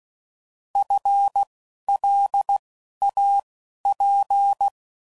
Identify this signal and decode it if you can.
Morse code, heard as a keyed tone, about 16 wpm, spaced as for 11 wpm, 800 Hz. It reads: FLAP